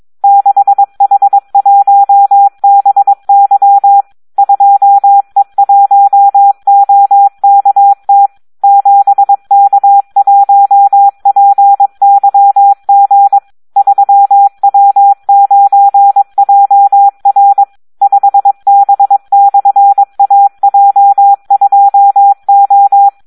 Morse code, heard as a keyed tone, about 22 wpm, 800 Hz. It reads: 6H1BY 2E1OKT 7K1PYG 3W9JR 5B/AJ2O
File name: Audio practica CW / SD 1 22